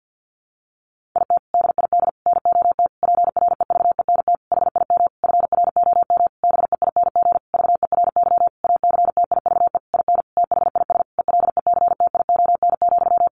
Morse code, heard as a keyed tone, about 50 wpm, 700 Hz. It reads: IT DID NOT PREVENT HIM FROM BEING VERY ACTIVE IN THIS ELECTION,
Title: Sentences from Top 1500 Words 50wpm